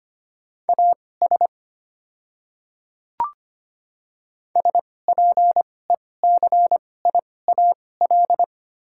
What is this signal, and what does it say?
Morse code, heard as a keyed tone, about 25 wpm, spaced as for 18 wpm, 700 Hz. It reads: AS SPECIAL
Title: 2x Character Spacing - Top 500 Words - 25wpm